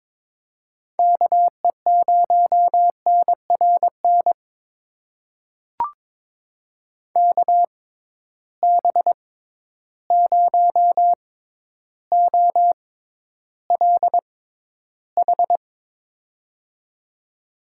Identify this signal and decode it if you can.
Morse code, heard as a keyed tone, about 22 wpm, spaced as for 11 wpm, 700 Hz. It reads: KE0NRN KB0OLH
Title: USA Call Signs - 6 Characters - 6x Character Spacing - 22wpm